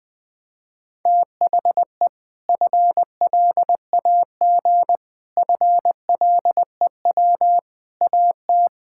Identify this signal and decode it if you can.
Morse code, heard as a keyed tone, about 20 wpm, 700 Hz. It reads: THE FLAG FLEW AT
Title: Sentences with 4-Word Phrases - Spoken Beginning - 20wpm